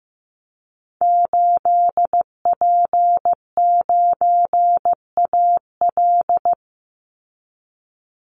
Morse code, heard as a keyed tone, about 15 wpm, 700 Hz. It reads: 8P9AL
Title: Call Signs Repeated 3x - 15wpm